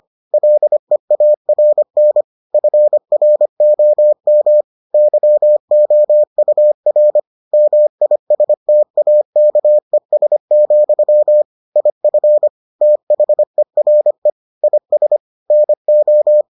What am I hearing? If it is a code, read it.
Morse code, heard as a keyed tone, about 25 wpm, 600 Hz. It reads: LEARN FROM YOUR MISTAKES, IF THERE IS NO